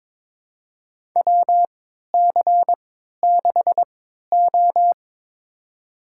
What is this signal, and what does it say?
Morse code, heard as a keyed tone, about 22 wpm, spaced as for 12 wpm, 700 Hz. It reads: WC6O
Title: USA Call Signs - 3x Character Spacing - 22wpm